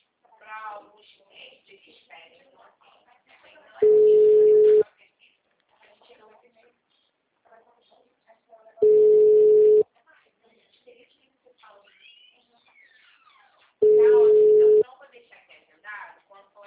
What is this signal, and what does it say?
Ringback tone (Europe, 425 Hz), with speech.